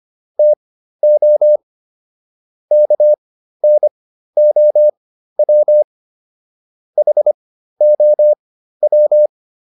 Morse code, heard as a keyed tone, about 25 wpm, spaced as for 13 wpm, 600 Hz. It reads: TO KNOW HOW